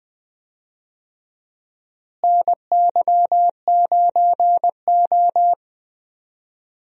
Morse code, heard as a keyed tone, about 21 wpm, 700 Hz. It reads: NY9O
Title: USA Call Signs - 4 Characters - Repeated 3x - 20wpm